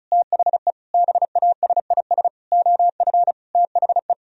Morse code, heard as a keyed tone, about 44 wpm, 700 Hz. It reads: THE BASIS OF THE